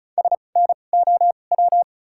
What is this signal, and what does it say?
Morse code, heard as a keyed tone, about 34 wpm, 700 Hz. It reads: S N O W
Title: 2x Character Spacing - Top 500 Words - 35wpm